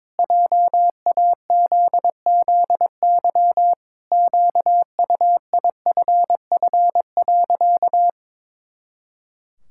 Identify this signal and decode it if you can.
Morse code, heard as a keyed tone, about 22 wpm, 700 Hz. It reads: JAZZY QUIFF.
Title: Pangram Challenge v2 - 22wpm